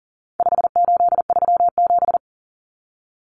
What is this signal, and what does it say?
Morse code, heard as a keyed tone, about 37 wpm, 700 Hz. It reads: 5837
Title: Numbers - 4-Digit 40wpm